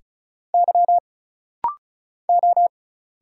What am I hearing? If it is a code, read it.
Morse code, heard as a keyed tone, about 37 wpm, 700 Hz. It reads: Y O
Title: Single Letters - Rapid-Fire 35wpm